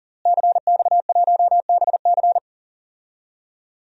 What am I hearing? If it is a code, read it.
Morse code, heard as a keyed tone, about 40 wpm, 700 Hz. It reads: CX1BC
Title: DXCC-100 Call Signs - 5 Characters - Repeated 2x - 40wpm